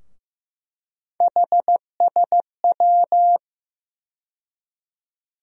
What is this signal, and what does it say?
Morse code, heard as a keyed tone, about 15 wpm, 700 Hz. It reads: HSW
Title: USA Call Sign Postfixes - 3 Letters 15wpm